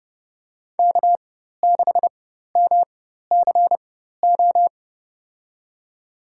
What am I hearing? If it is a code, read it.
Morse code, heard as a keyed tone, about 32 wpm, spaced as for 14 wpm, 700 Hz. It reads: K6MCO
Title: USA Call Signs - 5 Characters - 4x Character Spacing - 30wpm